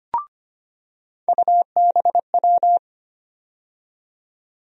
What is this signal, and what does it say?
Morse code, heard as a keyed tone, about 25 wpm, 700 Hz. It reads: UBW
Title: USA Call Sign Postfixes - 3 Letters 25wpm